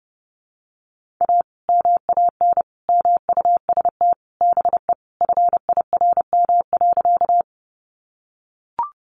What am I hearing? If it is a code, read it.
Morse code, heard as a keyed tone, about 30 wpm, 700 Hz. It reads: A MAN MUST BE FIRM.
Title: Sentences from Top 1400 Words 30wpm